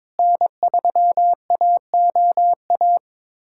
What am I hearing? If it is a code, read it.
Morse code, heard as a keyed tone, about 22 wpm, 700 Hz. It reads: N3AOA
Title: USA Callsign DE Callsign - 5 Characters - 22wpm